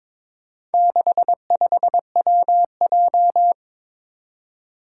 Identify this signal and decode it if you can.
Morse code, heard as a keyed tone, about 22 wpm, 700 Hz. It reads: 65WJ
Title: Four Challenging Letter-Number Combinations 22wpm